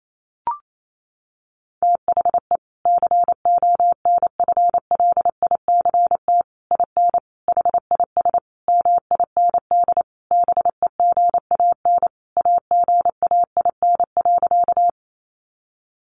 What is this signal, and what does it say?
Morse code, heard as a keyed tone, about 28 wpm, 700 Hz. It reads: THE CONFLICT IN HIS MIND BEGAN AGAIN.
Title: Sentences from Top 1300 Words 28wpm